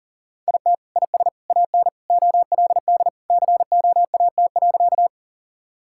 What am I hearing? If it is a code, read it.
Morse code, heard as a keyed tone, about 40 wpm, 700 Hz. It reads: IT IS AN OLD COAT.
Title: Sentences from Top 1600 Words 40wpm